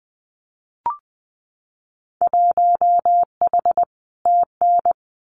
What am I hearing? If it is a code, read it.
Morse code, heard as a keyed tone, about 20 wpm, 700 Hz. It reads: 1H TN
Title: Winter Field Day Exchanges 20wpm